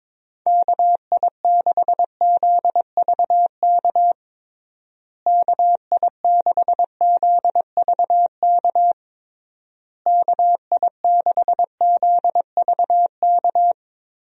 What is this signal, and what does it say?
Morse code, heard as a keyed tone, about 22 wpm, 700 Hz. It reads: KI6ZVK KI6ZVK KI6ZVK
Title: USA Call Signs - 6 Characters - Repeated 3x - 22wpm